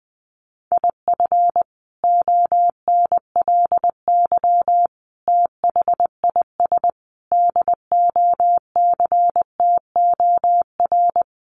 Morse code, heard as a keyed tone, about 20 wpm, 700 Hz. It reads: IF ONLY THIS DOCTOR
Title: Sentences from Top 1500 Words 20wpm